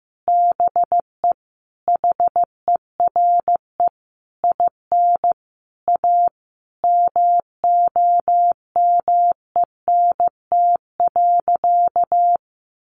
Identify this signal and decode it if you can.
Morse code, heard as a keyed tone, about 15 wpm, 700 Hz. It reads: BE HERE IN A MOMENT.
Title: Sentences from Top 1000 Words 15wpm